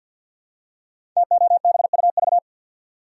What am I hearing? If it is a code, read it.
Morse code, heard as a keyed tone, about 49 wpm, 700 Hz. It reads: TOBUV
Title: Code Groups - 5 Letters - 50wpm